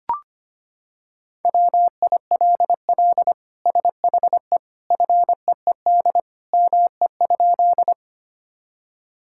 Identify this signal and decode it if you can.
Morse code, heard as a keyed tone, about 25 wpm, 700 Hz. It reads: WILL SHE FEED ME?
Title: Sentences from Top 1600 Words 25wpm